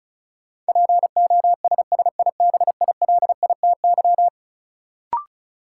Morse code, heard as a keyed tone, about 36 wpm, 700 Hz. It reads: POSSIBILITY
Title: Sets of 3 Words - New Word Drill - 35wpm